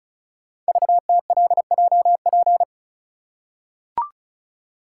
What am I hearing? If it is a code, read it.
Morse code, heard as a keyed tone, about 36 wpm, 690 Hz. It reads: VTLJP
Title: Code Groups - 5 Letters - 35wpm